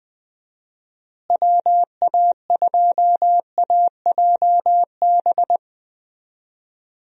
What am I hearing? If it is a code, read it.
Morse code, heard as a keyed tone, about 20 wpm, 700 Hz. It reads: WA2AJB
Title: USA Call Signs - 6 Characters - Repeated 3x - 20wpm